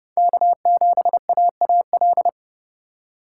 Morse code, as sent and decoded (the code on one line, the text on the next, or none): -.- --... .- .- .-..
K7AAL